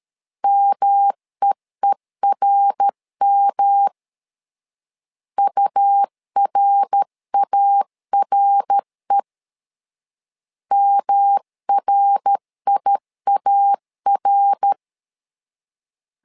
Morse code, heard as a keyed tone, about 13 wpm, 800 Hz. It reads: MEERM URARE MRIAR
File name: LEZIONI / 3 MWRO / 3 MWR 12wpm 1711